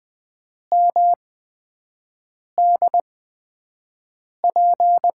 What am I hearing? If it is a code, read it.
Morse code, heard as a keyed tone, about 20 wpm, spaced as for 5 wpm, 700 Hz. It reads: MDP
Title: USA Call Signs - 5 Characters - 8x Character Spacing - 20wpm